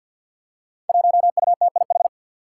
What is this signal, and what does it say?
Morse code, heard as a keyed tone, about 53 wpm, 700 Hz. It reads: 1UTIH
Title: Five Challenging Letter-Number Combinations 50wpm